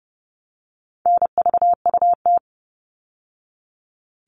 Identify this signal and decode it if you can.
Morse code, heard as a keyed tone, about 30 wpm, 700 Hz. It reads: NVUT
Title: All Cut Numbers - 4-Digit 30wpm